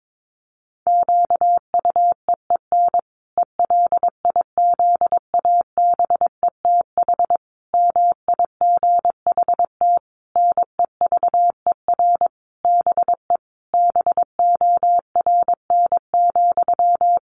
Morse code, heard as a keyed tone, about 22 wpm, 700 Hz. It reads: QUEEN ELIZABETH MIGHT NEVER BE BORN,
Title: A Sound Of Thunder 22wpm